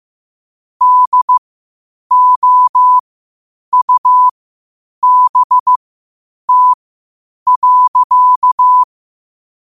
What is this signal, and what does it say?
Morse code, heard as a keyed tone, about 15 wpm, spaced as for 8 wpm, 1000 Hz. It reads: DOUBT.
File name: qotdfiles / 8wpm